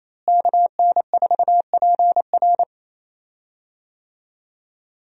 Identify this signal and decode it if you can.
Morse code, heard as a keyed tone, about 28 wpm, 700 Hz. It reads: KN4PR
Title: USA Call Signs - 5 Characters - Repeated 2x - 28wpm